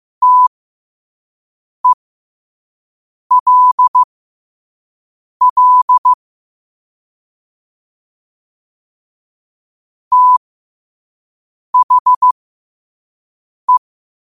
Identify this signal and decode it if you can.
Morse code, heard as a keyed tone, about 15 wpm, spaced as for 5 wpm, 1000 Hz. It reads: TELL THE
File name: qotdfiles / 5wpm